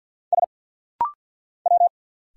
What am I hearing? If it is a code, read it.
Morse code, heard as a keyed tone, about 46 wpm, 700 Hz. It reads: S W
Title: Single Letters - Rapid-Fire 50wpm